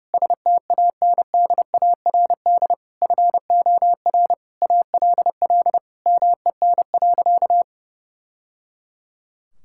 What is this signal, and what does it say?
Morse code, heard as a keyed tone, about 30 wpm, 700 Hz. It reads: STANDARD FOR ALL MEN.
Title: Sentences from Top 1000 Words - Review Part 6 - 30wpm